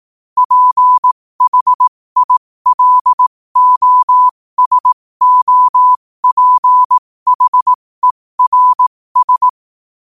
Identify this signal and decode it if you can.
Morse code, heard as a keyed tone, about 18 wpm, spaced as for 15 wpm, 1000 Hz. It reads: PHILOSOPHERS
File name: qotdfiles / 15wpm